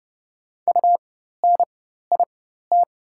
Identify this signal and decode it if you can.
Morse code, heard as a keyed tone, about 30 wpm, spaced as for 14 wpm, 700 Hz. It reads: UNIT
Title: 4x Character Spacing - Top 500 Words - 30wpm